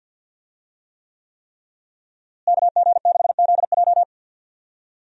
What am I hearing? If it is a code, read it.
Morse code, heard as a keyed tone, about 46 wpm, 700 Hz. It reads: KG6ZJ
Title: USA Call Signs - 5 Characters - Repeated 3x - 50wpm